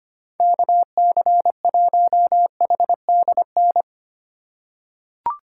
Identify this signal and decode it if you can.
Morse code, heard as a keyed tone, about 25 wpm, 700 Hz. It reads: KC1HDN
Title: USA Call Signs - 2x Character Spacing - 25wpm